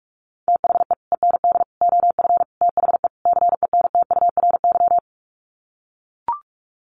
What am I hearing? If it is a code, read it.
Morse code, heard as a keyed tone, about 44 wpm, 690 Hz. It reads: THE END OF THE CENTURY
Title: Sets of 5 Words Encore2 45wpm